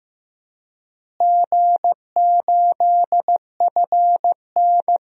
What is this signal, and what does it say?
Morse code, heard as a keyed tone, about 15 wpm, 700 Hz. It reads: G8FN
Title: DXCC-100 Call Signs - 4 Characters - Repeated 3x - 15wpm